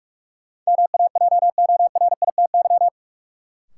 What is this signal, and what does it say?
Morse code, heard as a keyed tone, about 41 wpm, 700 Hz. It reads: MAJORITY
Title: Sets of 3 Words - New Word Drill - 45wpm